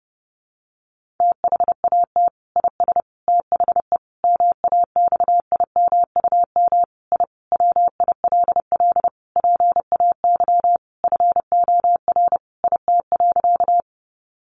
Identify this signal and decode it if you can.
Morse code, heard as a keyed tone, about 30 wpm, 700 Hz. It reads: THAT IS THE MAXIMUM I WILL PAY FOR IT.
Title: Sentences from Top 1300 Words 30wpm